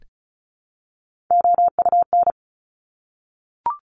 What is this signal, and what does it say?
Morse code, heard as a keyed tone, about 36 wpm, 700 Hz. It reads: OUN
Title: Top Three Letter Combinations in Words 35wpm